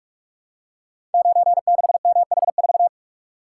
Morse code, heard as a keyed tone, about 45 wpm, 700 Hz. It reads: 96MH4
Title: Code Groups - 5 Letter:Numbers - 45wpm